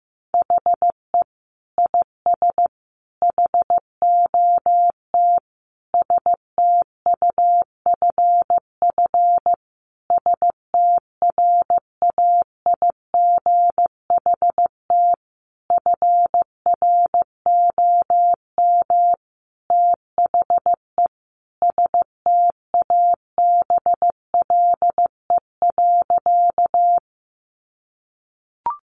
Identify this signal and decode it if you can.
Morse code, heard as a keyed tone, about 15 wpm, 700 Hz. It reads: HE IS HOT STUFF STRAIGHT FROM THE STABLE.
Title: Sentences from Top 1500 Words 15wpm